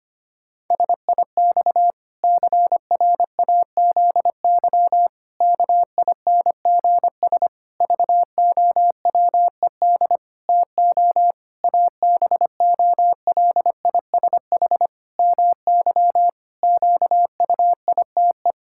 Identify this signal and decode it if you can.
Morse code, heard as a keyed tone, about 25 wpm, 700 Hz. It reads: SIX CRAZY KINGS VOWED TO ABOLISH MY QUITE
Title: Pangram Challenge v2 - 25wpm